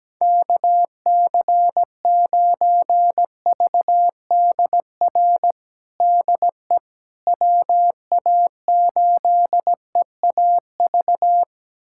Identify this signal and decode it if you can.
Morse code, heard as a keyed tone, about 17 wpm, 700 Hz. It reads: KC9VDR DE WA8EAV
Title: USA Callsign DE Callsign 17wpm